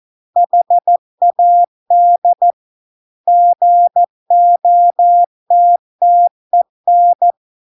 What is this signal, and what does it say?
Morse code, heard as a keyed tone, about 14 wpm, 700 Hz. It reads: HAD GOTTEN